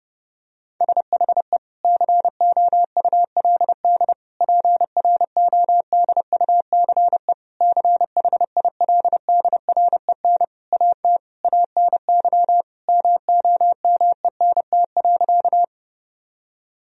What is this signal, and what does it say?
Morse code, heard as a keyed tone, about 30 wpm, 700 Hz. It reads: SHE COULD PRODUCE CHILDREN AT ANY MOMENT.
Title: Sentences from Top 1000 Words - Review Part 3 - 30wpm